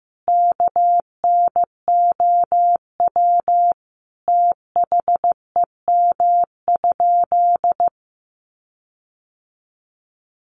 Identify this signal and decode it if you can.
Morse code, heard as a keyed tone, about 15 wpm, 700 Hz. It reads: KNOW THEM?
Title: Sentences from Top 1400 Words 15wpm